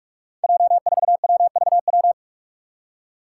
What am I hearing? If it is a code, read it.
Morse code, heard as a keyed tone, about 46 wpm, 700 Hz. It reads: J4WVW
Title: Five Challenging Letter-Number Combinations 45wpm